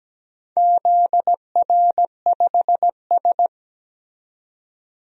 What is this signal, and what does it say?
Morse code, heard as a keyed tone, about 17 wpm, 700 Hz. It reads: ZR5S
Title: DXCC-100 Call Signs - 4 Characters - 17wpm